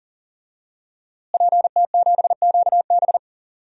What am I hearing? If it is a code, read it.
Morse code, heard as a keyed tone, about 37 wpm, 700 Hz. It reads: PT7QB
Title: DXCC-100 Call Signs - 5 Characters - Repeated 3x - 40wpm